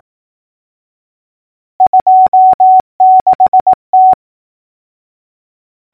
Morse code, heard as a keyed tone, about 18 wpm, 750 Hz. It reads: 26T
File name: numbers / V2-morse-clicks